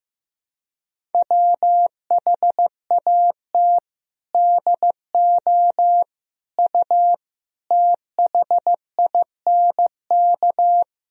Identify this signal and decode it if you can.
Morse code, heard as a keyed tone, about 15 wpm, 700 Hz. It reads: WHAT DO U THINK
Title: Sentences from Top 200 Words and Abbreviations 15wpm